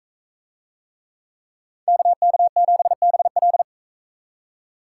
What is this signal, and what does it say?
Morse code, heard as a keyed tone, about 41 wpm, 700 Hz. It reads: KK7BL